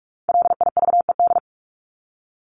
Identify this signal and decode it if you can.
Morse code, heard as a keyed tone, about 44 wpm, 700 Hz. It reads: LIVED
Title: ED Suffix - Top 100 Words with Base Words 45wpm